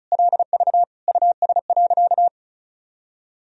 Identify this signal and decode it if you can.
Morse code, heard as a keyed tone, about 36 wpm, 700 Hz. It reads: LV US.
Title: Sentences from Top 400 Words and Abbreviations 35wpm